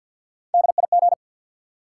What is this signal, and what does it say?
Morse code, heard as a keyed tone, about 52 wpm, 700 Hz. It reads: DIG
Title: Sentences from Top 1600 Words - New Word Drill - 50wpm